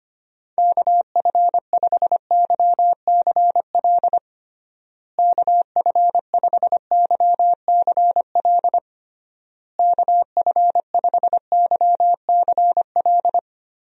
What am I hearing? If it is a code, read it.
Morse code, heard as a keyed tone, about 25 wpm, 700 Hz. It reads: KF5YCL KF5YCL KF5YCL
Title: USA Call Signs - 6 Characters - Repeated 3x - 25wpm